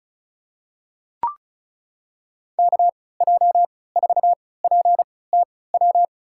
Morse code, heard as a keyed tone, about 36 wpm, spaced as for 20 wpm, 700 Hz. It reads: KJ4PTW
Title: USA Call Signs - 6 Characters - 3x Character Spacing - 35wpm